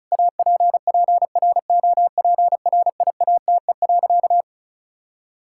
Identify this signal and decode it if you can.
Morse code, heard as a keyed tone, about 36 wpm, 700 Hz. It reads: APPROPRIATE.
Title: Sentences from Top 1000 Words - Review Part 5 - 35wpm